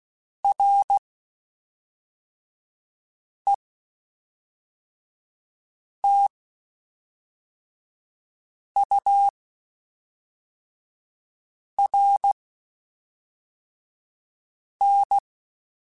Morse code, heard as a keyed tone, about 16 wpm, spaced as for 3 wpm, 800 Hz. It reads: RETURN
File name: (spaced) / QSOs Spaced-mp3s / kc7sac @5 sp